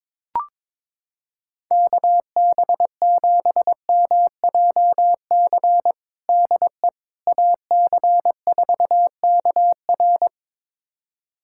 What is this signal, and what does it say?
Morse code, heard as a keyed tone, about 22 wpm, 700 Hz. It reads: KB7MJC DE AC4KR
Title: USA Callsign DE Callsign 22wpm